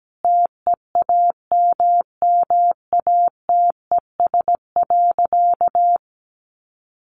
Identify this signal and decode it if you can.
Morse code, heard as a keyed tone, about 17 wpm, 700 Hz. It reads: TEAMMATES.
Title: Easy Phrases 17wpm